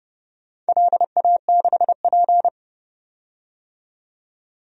Morse code, heard as a keyed tone, about 28 wpm, 700 Hz. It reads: LA6P
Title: DXCC-100 Call Signs - 4 Characters - 30wpm